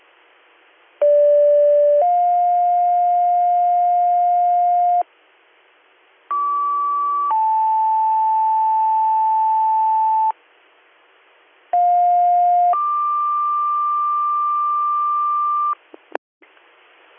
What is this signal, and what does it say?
Two-tone page, heard as a one-second tone, then a three-second tone, about 580 then 710 Hz, about 1150 then 880 Hz, about 710 then 1180 Hz. It is heard from 1.0 to 15.7 s.